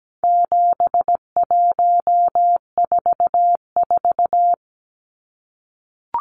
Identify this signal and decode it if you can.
Morse code, heard as a keyed tone, about 17 wpm, 700 Hz. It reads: 7144
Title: Numbers - 4-Digit 17wpm